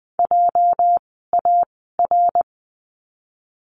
Morse code, heard as a keyed tone, about 20 wpm, spaced as for 14 wpm, 700 Hz. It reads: JAR
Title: 206 Lesson 33f - Letter j - Words 20wpm